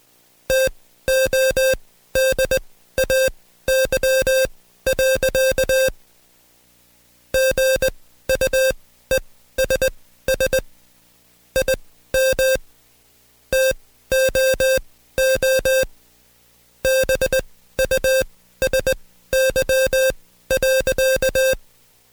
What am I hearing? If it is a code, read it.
Morse code, heard as a keyed tone, about 20 wpm, spaced as for 13 wpm, 530 Hz. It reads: TODAY. GUESS IM TOO BUSY.
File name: Code Quick / AudioDownld2 / Seg133-Stepper6 13WPM